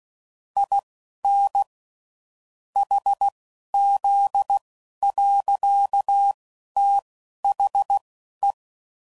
Morse code, heard as a keyed tone, about 16 wpm, spaced as for 11 wpm, 800 Hz. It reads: IN HZ. THE
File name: CW Code Groups @10WPM / Technical Subjects / XC